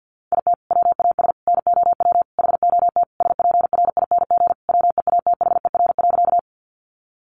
Morse code, heard as a keyed tone, about 48 wpm, 700 Hz. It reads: IT WAS NOW HOT SPRING WEATHER.
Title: Sentences from Top 1200 Words 50wpm